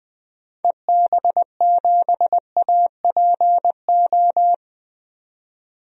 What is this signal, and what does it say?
Morse code, heard as a keyed tone, about 20 wpm, 700 Hz. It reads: EB7APO
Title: DXCC-100 Call Signs - 6 Characters - 20wpm